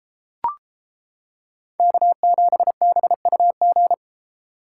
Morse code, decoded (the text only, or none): K7BUG